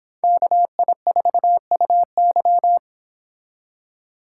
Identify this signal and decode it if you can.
Morse code, heard as a keyed tone, about 26 wpm, 700 Hz. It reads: KI4UY